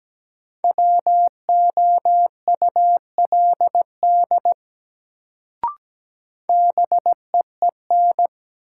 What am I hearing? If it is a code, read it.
Morse code, heard as a keyed tone, about 17 wpm, 700 Hz. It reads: WOULD BEEN
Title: Sentences from Top 100 Words - New Word Drill - 17wpm